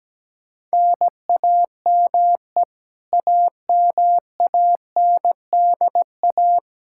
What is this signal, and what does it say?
Morse code, heard as a keyed tone, about 17 wpm, 700 Hz. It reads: NAME AMANDA
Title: K1USN SST QSO Practice 17wpm